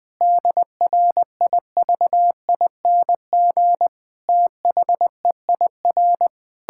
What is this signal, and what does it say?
Morse code, heard as a keyed tone, about 20 wpm, 700 Hz. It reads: DRIVING THEIR
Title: Pangram Challenge v2 - 20wpm